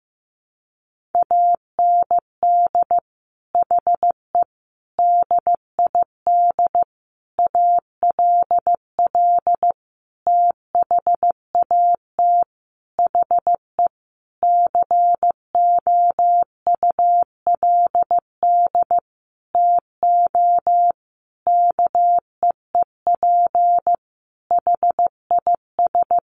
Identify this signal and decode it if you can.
Morse code, heard as a keyed tone, about 15 wpm, 700 Hz. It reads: AND HE DID ALL THAT HE COULD TO KEEP HIS
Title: Sentences from Top 1500 Words 15wpm